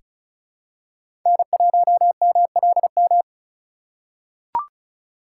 Morse code, heard as a keyed tone, about 34 wpm, 700 Hz. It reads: N1MLM
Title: USA Call Signs - 5 Characters - 4x Character Spacing - 35wpm